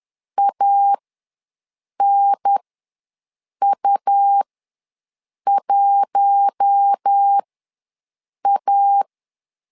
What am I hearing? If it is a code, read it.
Morse code, heard as a keyed tone, about 10 wpm, spaced as for 6 wpm, 800 Hz. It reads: ANU1A
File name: LEZIONI / 4 Q1 ws 10wpm 2538